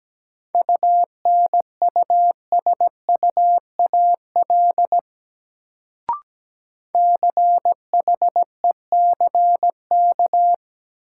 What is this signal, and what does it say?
Morse code, heard as a keyed tone, about 17 wpm, 700 Hz. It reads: UNUSUAL CHECK
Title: Sentences from Top 1400 Words - New Word Drill Alt - 17wpm